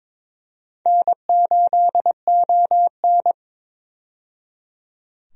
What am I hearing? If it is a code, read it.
Morse code, heard as a keyed tone, about 22 wpm, 700 Hz. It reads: N8ON
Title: USA Call Signs - 4 Characters - 22wpm